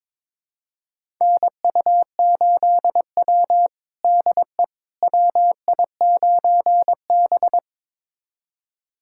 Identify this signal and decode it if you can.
Morse code, heard as a keyed tone, about 22 wpm, 700 Hz. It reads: NU8W DE WI9B
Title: USA Callsign DE Callsign - 4 Characters - 22wpm